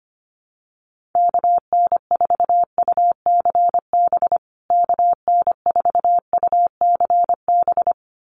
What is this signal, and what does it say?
Morse code, heard as a keyed tone, about 25 wpm, 700 Hz. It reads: KN4UCB KN4UCB
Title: USA Call Signs Repeated 2x 25wpm